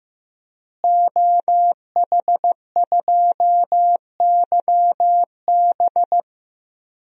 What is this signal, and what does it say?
Morse code, heard as a keyed tone, about 15 wpm, 700 Hz. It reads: OH2YB
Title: DXCC-100 Call Signs - 5 Characters - Repeated 2x - 15wpm